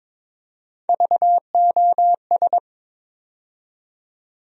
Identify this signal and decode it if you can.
Morse code, heard as a keyed tone, about 22 wpm, 700 Hz. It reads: VOS
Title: USA Call Sign Postfixes - 3 Letters 22wpm